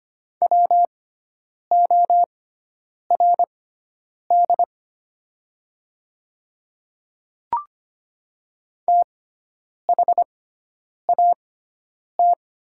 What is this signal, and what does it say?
Morse code, heard as a keyed tone, about 25 wpm, spaced as for 9 wpm, 700 Hz. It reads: WORD THAT
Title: 6x Character Spacing - Top 100 Words - 25wpm